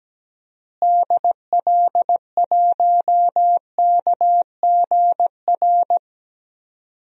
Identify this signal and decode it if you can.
Morse code, heard as a keyed tone, about 17 wpm, 700 Hz. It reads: DL1KGR
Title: DXCC-100 Call Signs - 6 Characters - Repeated 3x - 17wpm